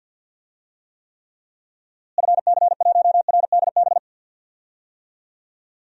Morse code, heard as a keyed tone, about 47 wpm, 700 Hz. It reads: FC1RDB
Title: DXCC-100 Call Signs - 6 Characters - Repeated 3x - 50wpm